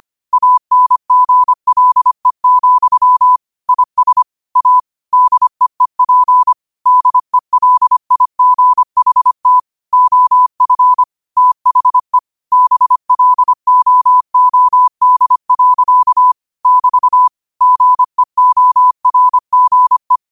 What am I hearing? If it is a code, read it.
Morse code, heard as a keyed tone, about 25 wpm, 1000 Hz. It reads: ANGLE, IS A DEEP DELIGHT OF THE BLOOD. <BT> GEORGE
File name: qotdfiles / 25wpm